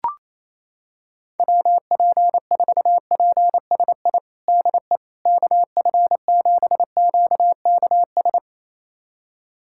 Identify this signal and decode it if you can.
Morse code, heard as a keyed tone, about 28 wpm, 700 Hz. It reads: WP4PSI DE KF7QKS
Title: USA Callsign DE Callsign - 6 Characters - 28wpm